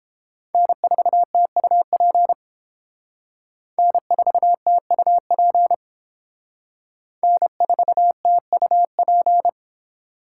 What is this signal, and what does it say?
Morse code, heard as a keyed tone, about 29 wpm, 700 Hz. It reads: N4TUP N4TUP N4TUP